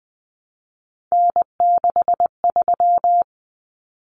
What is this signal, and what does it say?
Morse code, heard as a keyed tone, about 20 wpm, 700 Hz. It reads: N63
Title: Common Cut Numbers - 3-Digit 20wpm